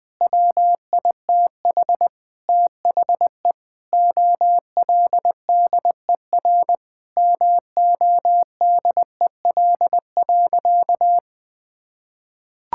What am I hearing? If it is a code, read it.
Morse code, heard as a keyed tone, about 20 wpm, 700 Hz. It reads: WITH THE OLDER MODEL.
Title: Sentences with 7-Word Phrases - 20wpm